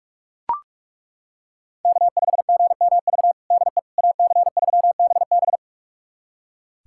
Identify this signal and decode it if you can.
Morse code, heard as a keyed tone, about 42 wpm, 700 Hz. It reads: K5GMV DE AC3BB
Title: USA Callsign DE Callsign - 5 Characters - 45wpm